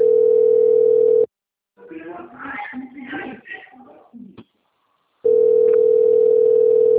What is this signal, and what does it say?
Ringback tone (North America), 440 + 480 Hz, with speech.
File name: rg-1600-unknown- - - .11570